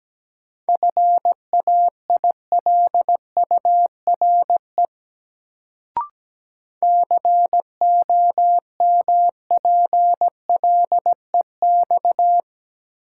Morse code, heard as a keyed tone, about 17 wpm, 700 Hz. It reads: FAILURE COMPLEX
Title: Sentences from Top 900 Words - New Word Drill - 17wpm